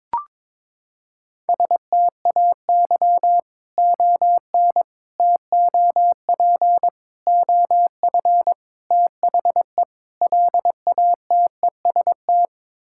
Morse code, heard as a keyed tone, about 22 wpm, 700 Hz. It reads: STAY ON TOP OF THE LATEST
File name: practice / 22wpm / Sets of 6 Words - 22wpm